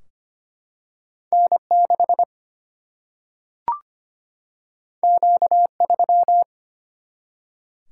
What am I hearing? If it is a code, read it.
Morse code, heard as a keyed tone, about 25 wpm, 700 Hz. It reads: N6 Q3
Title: Single-Letter Single-Number 25wpm